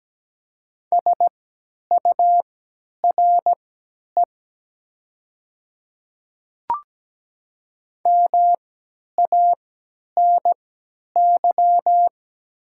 Morse code, heard as a keyed tone, about 17 wpm, spaced as for 10 wpm, 700 Hz. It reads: SURE MANY
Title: 3x Character Spacing - Top 500 Words - 17wpm